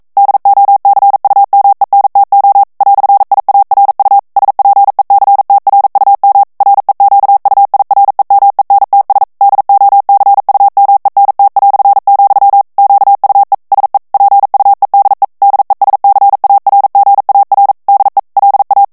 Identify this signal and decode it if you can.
Morse code, heard as a keyed tone, about 42 wpm, 800 Hz. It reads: DOCUMENTO "IARU SPECTRUM REQUIREMENTS DOCUMENT", QUE SE PUEDE DESCARGAR DE LA
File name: morse mp3 / ESPAGNOL TEXTE EN CLAIR 1 VITESSE 40 MOTS MINUTES